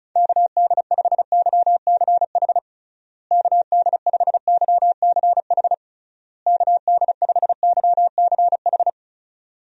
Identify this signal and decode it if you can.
Morse code, heard as a keyed tone, about 36 wpm, 700 Hz. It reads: KD5YCH KD5YCH KD5YCH
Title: USA Call Signs - 6 Characters - Repeated 3x - 35wpm